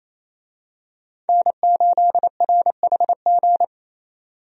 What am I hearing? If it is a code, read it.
Morse code, heard as a keyed tone, about 27 wpm, 700 Hz. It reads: N8RHG
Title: USA Call Signs - 5 Characters - 28wpm